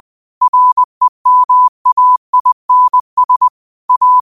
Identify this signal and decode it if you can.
Morse code, heard as a keyed tone, about 20 wpm, 1000 Hz. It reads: REMAINS A
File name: qotdfiles / 20wpm